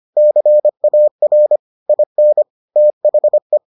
Morse code, heard as a keyed tone, about 25 wpm, 600 Hz. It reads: CAR IN THE